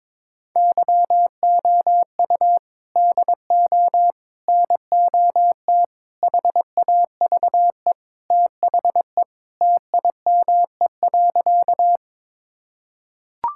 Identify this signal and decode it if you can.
Morse code, heard as a keyed tone, about 22 wpm, 700 Hz. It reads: YOU DO NOT HAVE THE TIME.
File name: practice / 22wpm / Sentences with 4-Word Phrases - 22wpm